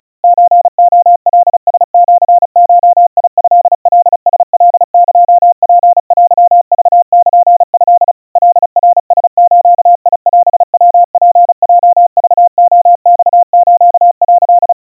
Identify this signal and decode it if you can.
Morse code (Japanese Wabun code), heard as a keyed tone, about 36 wpm, 700 Hz. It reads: ソレカラシゴトカラカエツテクルト カナラズオヤツヲクレマス」
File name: kawa / Morse sample / kamyu 130 cpm 700 Hz Crispy